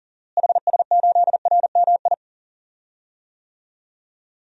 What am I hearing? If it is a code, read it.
Morse code, heard as a keyed tone, about 40 wpm, 700 Hz. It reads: HS8RMI
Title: DXCC-100 Call Signs - 6 Characters - Repeated 3x - 40wpm